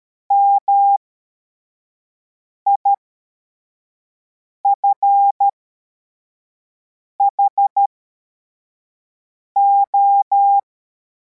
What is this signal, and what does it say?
Morse code, heard as a keyed tone, about 13 wpm, spaced as for 4 wpm, 800 Hz. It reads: MIFHO
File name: formation / cw / Text84